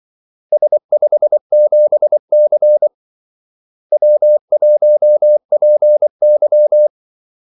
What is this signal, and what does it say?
Morse code, heard as a keyed tone, about 24 wpm, 600 Hz. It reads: S57C W1PY
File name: Callsigns-24wpm